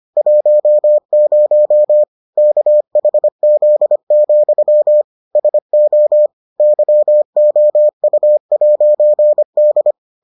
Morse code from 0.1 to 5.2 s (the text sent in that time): 10 KHZ,